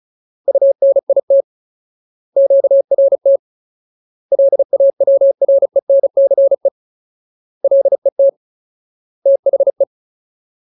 Morse code, heard as a keyed tone, about 35 wpm, 550 Hz. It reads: UNIT QRT LAWRENCE LET THE
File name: Ultimate Word List / 35WPM Ultimate